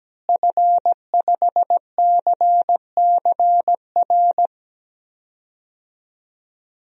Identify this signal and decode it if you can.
Morse code, heard as a keyed tone, about 17 wpm, 700 Hz. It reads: F5CCR
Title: DXCC-100 Call Signs - 5 Characters - Repeated 3x - 17wpm